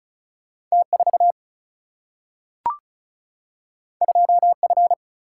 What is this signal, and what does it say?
Morse code, heard as a keyed tone, about 36 wpm, 700 Hz. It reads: T4 2F
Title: Single-Letter Single-Number 35wpm